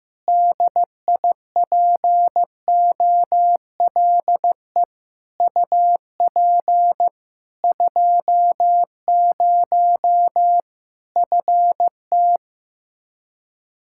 Morse code, heard as a keyed tone, about 15 wpm, 700 Hz. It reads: DIPOLE UP 20 FT
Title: Practice QSO - Exchange 2 - Info - 15wpm